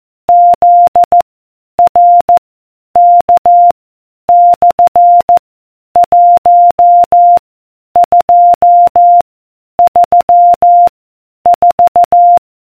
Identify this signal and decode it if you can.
Morse code, heard as a keyed tone, about 14 wpm, 700 Hz. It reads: Z R K / 1 2 3 4